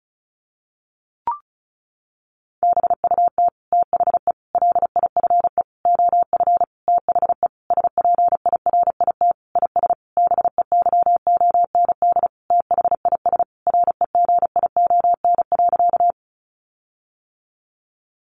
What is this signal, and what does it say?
Morse code, heard as a keyed tone, about 34 wpm, 700 Hz. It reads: BUT THE LIFE OF THE SPIRIT IS BEYOND THIS REGION.
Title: Sentences from Top 1500 Words 35wpm